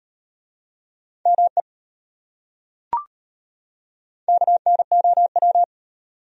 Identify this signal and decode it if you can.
Morse code, heard as a keyed tone, about 33 wpm, 700 Hz. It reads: ME KNOW